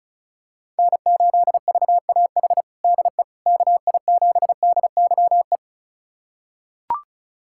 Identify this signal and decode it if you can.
Morse code, heard as a keyed tone, about 34 wpm, 700 Hz. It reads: N8VAH DE KI7DYE
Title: USA Callsign DE Callsign 35wpm